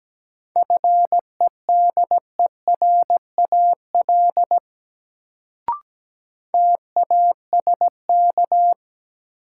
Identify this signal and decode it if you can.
Morse code, heard as a keyed tone, about 17 wpm, 700 Hz. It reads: FEDERAL TASK
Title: Sentences from Top 1100 Words - New Word Drill Alt - 17wpm